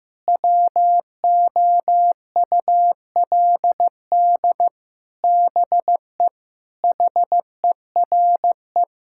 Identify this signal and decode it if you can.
Morse code, heard as a keyed tone, about 14 wpm, 700 Hz. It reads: WOULD BE HERE
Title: Sentences from Top 1600 Words 15wpm